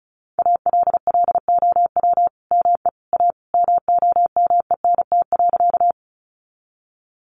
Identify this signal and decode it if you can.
Morse code, heard as a keyed tone, about 34 wpm, 700 Hz. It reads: ALLOW ME A MOMENT.
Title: Sentences from Top 1000 Words 35wpm